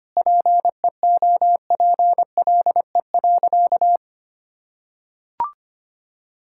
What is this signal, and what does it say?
Morse code, heard as a keyed tone, about 25 wpm, 700 Hz. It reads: PEOPLE.
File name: practice / 25wpm / Sentences with 9-Word Phrases - 25wpm